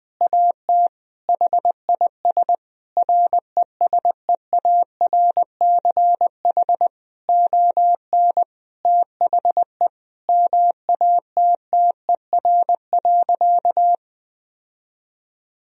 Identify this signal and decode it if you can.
Morse code, heard as a keyed tone, about 19 wpm, 700 Hz. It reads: AT HIS RESEARCH ON THE MATTER.
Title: Sentences from Top 1000 Words - Review Part 4 - 20wpm